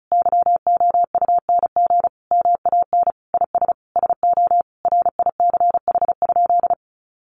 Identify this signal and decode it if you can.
Morse code, heard as a keyed tone, about 34 wpm, 700 Hz. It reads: YOUNG MAN IS SO RICH?
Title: Sentences from Top 1100 Words 35wpm